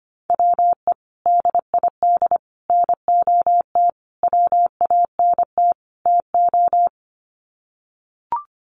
Morse code, heard as a keyed tone, about 25 wpm, 690 Hz. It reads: WE DID NOT WANT TO
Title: Sets of 5 Words Encore 25wpm